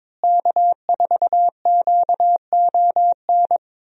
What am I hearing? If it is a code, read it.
Morse code, heard as a keyed tone, about 22 wpm, 700 Hz. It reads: K4QON
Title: USA Call Signs - 5 Characters - Repeated 3x - 22wpm